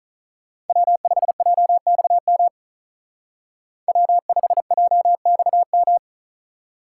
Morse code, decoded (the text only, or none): W5JXM W5JXM